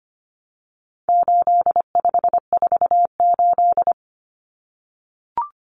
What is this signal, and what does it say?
Morse code, heard as a keyed tone, about 25 wpm, 700 Hz. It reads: 8548
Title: Numbers - 4-Digit 25wpm